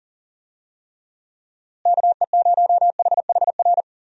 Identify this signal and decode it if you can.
Morse code, heard as a keyed tone, about 40 wpm, 700 Hz. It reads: KE0HHR
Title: USA Call Signs - 6 Characters - Repeated 3x - 40wpm